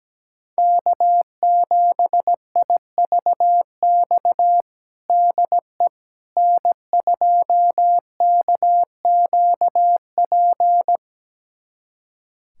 Morse code, heard as a keyed tone, about 17 wpm, 700 Hz. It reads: K7IVX DE N2KQP
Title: USA Callsign DE Callsign - 5 Characters - 17wpm